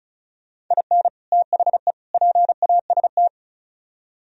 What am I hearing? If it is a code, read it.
Morse code, heard as a keyed tone, about 46 wpm, 700 Hz. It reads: IN THE PAST